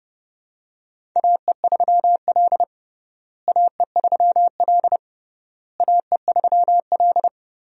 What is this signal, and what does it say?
Morse code, heard as a keyed tone, about 30 wpm, 700 Hz. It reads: AE3L AE3L AE3L
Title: USA Call Signs - 4 Characters - Repeated 3x - 30wpm